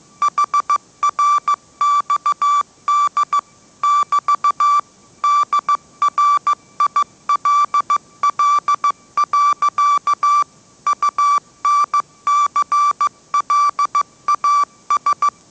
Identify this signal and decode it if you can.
Morse code, heard as a keyed tone, about 15 wpm, 1160 Hz. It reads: HRXD <BT> DRILL. UNCLAS
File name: mrx / mrx065 16wpm